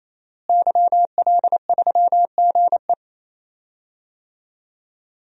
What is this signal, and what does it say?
Morse code, heard as a keyed tone, about 28 wpm, 700 Hz. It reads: YL3GE
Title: DXCC-100 Call Signs - 5 Characters - Repeated 3x - 28wpm